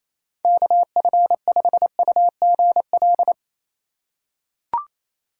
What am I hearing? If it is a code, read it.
Morse code, heard as a keyed tone, about 28 wpm, 700 Hz. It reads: KF5UGL
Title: USA Call Signs - 2x Character Spacing - 28wpm